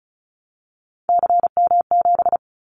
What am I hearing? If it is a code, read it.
Morse code, heard as a keyed tone, about 34 wpm, 700 Hz. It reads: CM7
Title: Callsign Prefixes for Top 100 DXCC - 35wpm